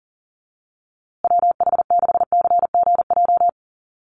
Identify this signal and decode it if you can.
Morse code, heard as a keyed tone, about 40 wpm, 700 Hz. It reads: WH6CGJ
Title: USA Call Signs Repeated 3x 40wpm